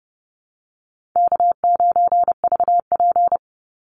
Morse code, heard as a keyed tone, about 30 wpm, 700 Hz. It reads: K9VP
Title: Easy Call Signs Repeated 2x - 30wpm